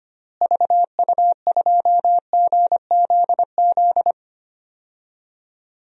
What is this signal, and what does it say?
Morse code, heard as a keyed tone, about 25 wpm, 700 Hz. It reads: VU2GZZ
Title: DXCC-100 Call Signs - 6 Characters - Repeated 3x - 25wpm